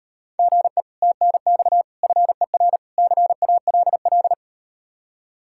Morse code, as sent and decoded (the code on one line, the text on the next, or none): --. / - -. -..- / ..-. .-. / -.-. .- .-.. .-..
GE TNX FER CALL